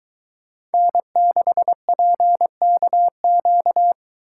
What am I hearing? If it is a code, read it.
Morse code, heard as a keyed tone, about 21 wpm, 700 Hz. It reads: N6PKQ